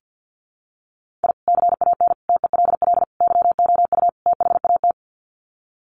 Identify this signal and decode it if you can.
Morse code, heard as a keyed tone, about 48 wpm, 700 Hz. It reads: I CAN TELL YOU THAT
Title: Sets of 5 Words 50wpm